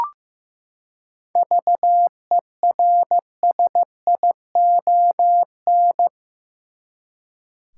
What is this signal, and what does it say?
Morse code, heard as a keyed tone, about 15 wpm, 700 Hz. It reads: VERSION
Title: Seven Letter Words - 15wpm